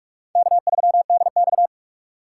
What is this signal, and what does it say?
Morse code, heard as a keyed tone, about 64 wpm, spaced as for 54 wpm, 700 Hz. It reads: K3DX
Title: Speed Racing - USA 1x2 Call Signs 45wpm